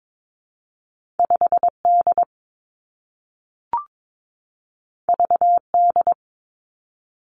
Morse code, heard as a keyed tone, about 22 wpm, 700 Hz. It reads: 5 D V D
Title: Two Challenging Letter-Number Combinations 22wpm